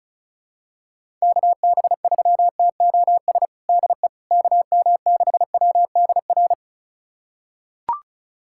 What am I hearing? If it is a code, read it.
Morse code, heard as a keyed tone, about 34 wpm, 700 Hz. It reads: KB3TOS DE KM6WDR
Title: USA Callsign DE Callsign - 6 Characters - 35wpm